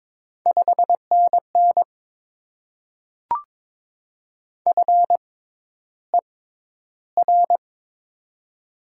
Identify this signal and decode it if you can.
Morse code, heard as a keyed tone, about 22 wpm, spaced as for 11 wpm, 700 Hz. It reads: 5NN FER
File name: Top 20 QSO Elements - 6x Character Spacing - 22wpm